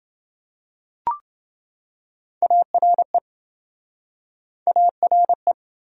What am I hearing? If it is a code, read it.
Morse code, heard as a keyed tone, about 28 wpm, 700 Hz. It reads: ARE ARE